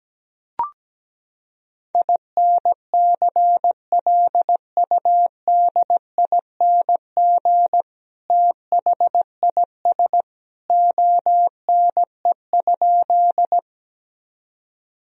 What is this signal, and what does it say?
Morse code, heard as a keyed tone, about 17 wpm, 700 Hz. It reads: INCLUDING THIS ONE?
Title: Sentences from Top 1000 Words - Review Part 4 - 17wpm